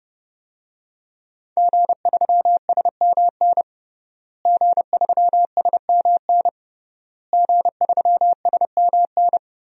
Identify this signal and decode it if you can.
Morse code, heard as a keyed tone, about 32 wpm, 700 Hz. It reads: G3SMN G3SMN G3SMN
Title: DXCC-100 Call Signs - 5 Characters - Repeated 3x - 30wpm